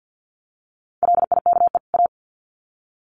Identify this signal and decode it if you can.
Morse code, heard as a keyed tone, about 50 wpm, 700 Hz. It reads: LIKE A
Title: Sets of 2 Words 50wpm